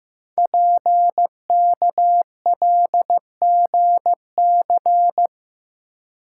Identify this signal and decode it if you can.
Morse code, heard as a keyed tone, about 15 wpm, 700 Hz. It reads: PKLGC
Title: Code Groups - 5 Letter:Numbers - 15wpm